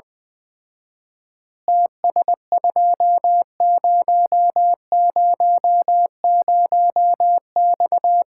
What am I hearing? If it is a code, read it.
Morse code, heard as a keyed tone, about 21 wpm, 700 Hz. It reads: TS2000X
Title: Common QSO Phrases - 4x Word-Spacing - 20wpm